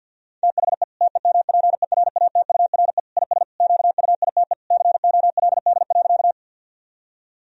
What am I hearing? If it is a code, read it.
Morse code, heard as a keyed tone, about 50 wpm, 700 Hz. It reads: THE TEMPERATURE IS QUITE COLD.
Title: Sentences from Top 1000 Words - Review Part 4 - 50wpm